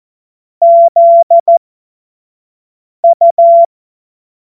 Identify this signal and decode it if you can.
Morse code, heard as a keyed tone, about 14 wpm, spaced as for 5 wpm, 680 Hz. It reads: ZU
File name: Morsekurs / Witze / Witz 17. August 2021, 25 F70